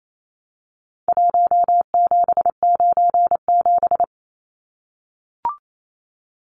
Morse code, heard as a keyed tone, about 28 wpm, 700 Hz. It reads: 1797
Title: Numbers - Contest Style - 4-Digit 28wpm